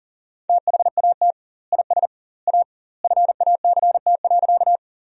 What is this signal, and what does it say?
Morse code, heard as a keyed tone, about 40 wpm, 700 Hz. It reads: THAT IS A FACT.
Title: 2x Word-Spacing - Sentences from Top 500 Words - 40wpm